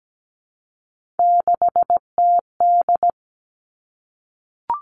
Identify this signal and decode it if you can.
Morse code, heard as a keyed tone, about 17 wpm, 700 Hz. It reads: 6TD
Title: Three Challenging Letter-Number Combinations 17wpm